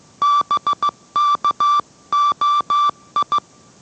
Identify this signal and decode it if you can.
Morse code, heard as a keyed tone, about 15 wpm, 1160 Hz. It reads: BKOI